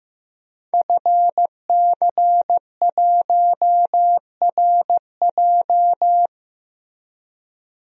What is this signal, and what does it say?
Morse code, heard as a keyed tone, about 15 wpm, 700 Hz. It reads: FC1RJ
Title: DXCC-100 Call Signs - 5 Characters - Repeated 2x - 15wpm